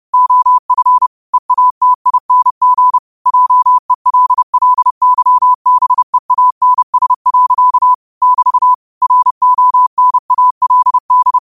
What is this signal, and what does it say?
Morse code, heard as a keyed tone, about 30 wpm, 1000 Hz. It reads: OF EATING JELLYBEANS. <BT> RONALD